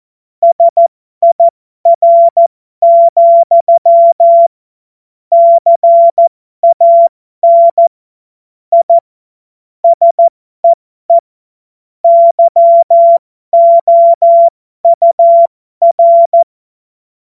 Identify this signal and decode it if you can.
Morse code, heard as a keyed tone, about 14 wpm, spaced as for 12 wpm, 680 Hz. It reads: SIR, CAN I SEE YOUR
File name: Morsekurs / Witze